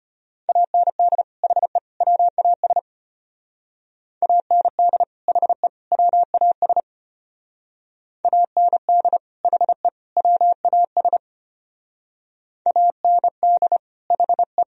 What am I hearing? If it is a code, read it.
Morse code, heard as a keyed tone, about 32 wpm, 700 Hz. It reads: AND HE WAS AND HE WAS AND HE WAS AND HE